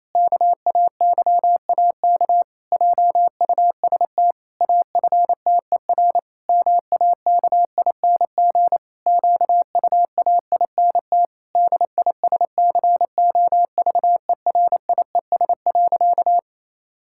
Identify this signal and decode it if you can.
Morse code, heard as a keyed tone, about 28 wpm, 700 Hz. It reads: KAYAK JUST AFTER MAKING QUAINT DISCOVERIES.
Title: Pangram Challenge v2 - 28wpm